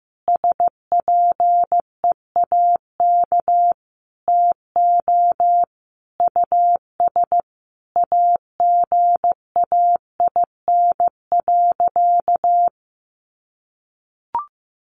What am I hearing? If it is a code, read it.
Morse code, heard as a keyed tone, about 15 wpm, 700 Hz. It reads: SPEAK TO US AGAIN.
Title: Sentences from Top 1100 Words 15wpm